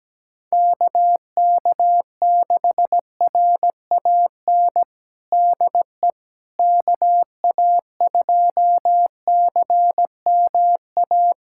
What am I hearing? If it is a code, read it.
Morse code, heard as a keyed tone, about 17 wpm, 700 Hz. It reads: KK6RAN DE KA2CMA
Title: USA Callsign DE Callsign - 6 Characters - 17wpm